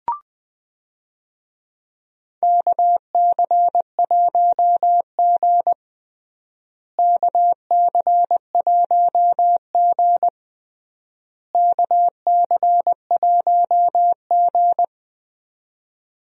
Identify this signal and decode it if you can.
Morse code, heard as a keyed tone, about 20 wpm, 700 Hz. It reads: KC1G KC1G KC1G
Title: USA Call Signs - 4 Characters - Repeated 3x - 20wpm